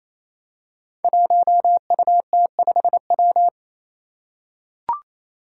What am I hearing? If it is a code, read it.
Morse code, heard as a keyed tone, about 28 wpm, 700 Hz. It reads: 1UT5W
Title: Five Challenging Letter-Number Combinations 28wpm